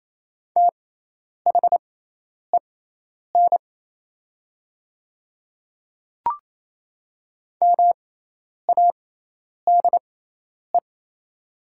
Morse code, heard as a keyed tone, about 28 wpm, spaced as for 10 wpm, 690 Hz. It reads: THEN MADE
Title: 6x Character Spacing - Top 200 Words - 28wpm